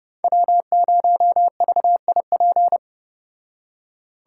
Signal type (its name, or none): Morse code